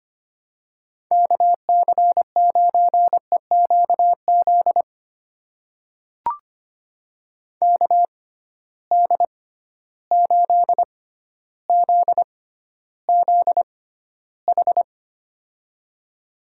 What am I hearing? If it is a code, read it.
Morse code, heard as a keyed tone, about 25 wpm, spaced as for 13 wpm, 700 Hz. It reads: KC9EQZ KD8ZZH
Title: USA Call Signs - 6 Characters - 6x Character Spacing - 25wpm